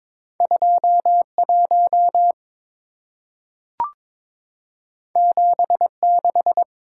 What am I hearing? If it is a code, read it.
Morse code, heard as a keyed tone, about 23 wpm, 700 Hz. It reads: 21 76
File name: practice / 22wpm / Two Letter-Number Combinations - 22wpm